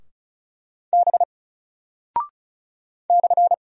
Morse code, heard as a keyed tone, about 36 wpm, 700 Hz. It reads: B /
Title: Single Character (Letter:Number:Punctuation:Prosign) Alt 35wpm